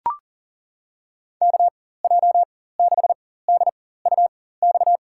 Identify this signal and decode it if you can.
Morse code, heard as a keyed tone, about 40 wpm, spaced as for 19 wpm, 700 Hz. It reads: KJ6DUX
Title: USA Call Signs - 6 Characters - 4x Character Spacing - 40wpm